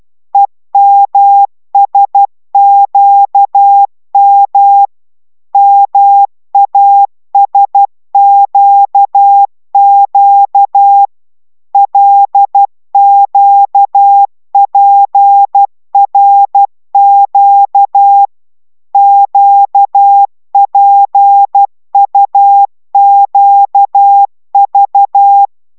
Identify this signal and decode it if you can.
Morse code, heard as a keyed tone, about 12 wpm, 800 Hz. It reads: EMSQM MASQQ LQPRQ QPUQV
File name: taviro tanulas / cw 69 16 10